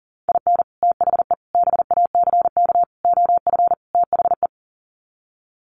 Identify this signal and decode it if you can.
Morse code, heard as a keyed tone, about 40 wpm, 700 Hz. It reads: IN THE BACK OF THE
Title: Sets of 5 Words 40wpm